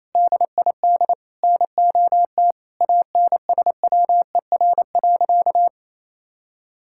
Morse code, heard as a keyed tone, about 28 wpm, 700 Hz. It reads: DID NOT ANSWER.
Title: Sentences from Top 1000 Words - Review Part 6 - 28wpm